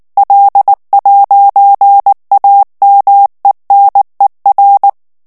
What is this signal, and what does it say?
Morse code, heard as a keyed tone, about 19 wpm, 800 Hz. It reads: L'AMENER
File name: FRANCAIS TEXTE CLAIR 2 VITESSE 18 MOTS MINUTES